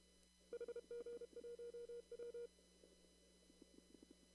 Morse code, heard as a keyed tone, about 32 wpm, 480 Hz. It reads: HG1W
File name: CW / kt5j wpx cw-op-k5pi